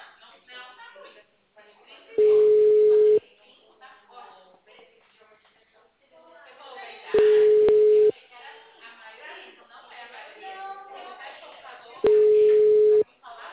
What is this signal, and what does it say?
Ringback tone (Europe, 425 Hz), with speech.
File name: out- -2011- - - .1458